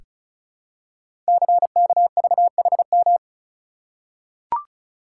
Morse code, heard as a keyed tone, about 34 wpm, 690 Hz. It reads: CKVHM
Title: Code Groups - 5 Letters - 35wpm